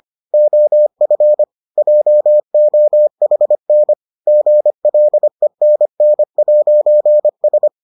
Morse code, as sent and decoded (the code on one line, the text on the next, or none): --- ..-. / .--- --- .... -. / --. .-.. -. -. .----. ...
OF JOHN GLENN'S